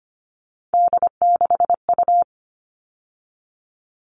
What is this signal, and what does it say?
Morse code, heard as a keyed tone, about 25 wpm, 700 Hz. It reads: D6U
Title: All Cut Numbers - 3-Digit 25wpm